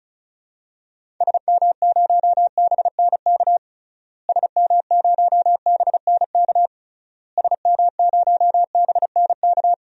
Morse code, heard as a keyed tone, about 34 wpm, 700 Hz. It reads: SM0BNK SM0BNK SM0BNK
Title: DXCC-100 Call Signs - 6 Characters - Repeated 3x - 35wpm